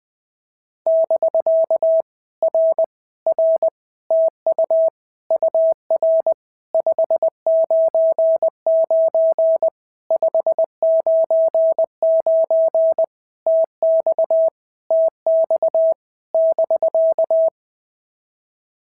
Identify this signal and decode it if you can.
Morse code, heard as a keyed tone, about 20 wpm, 650 Hz. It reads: <BK> R R TU UR 599 599 TX TX <BK>
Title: POTA QSO Practice - 20wpm